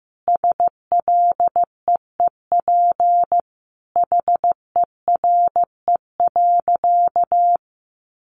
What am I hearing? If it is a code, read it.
Morse code, heard as a keyed tone, about 15 wpm, 700 Hz. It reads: SLEEP HERE.
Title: Sentences from Top 500 Words 15wpm